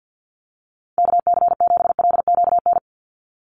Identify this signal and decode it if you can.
Morse code, heard as a keyed tone, about 49 wpm, 700 Hz. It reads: KC7LQN
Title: USA Call Signs 50wpm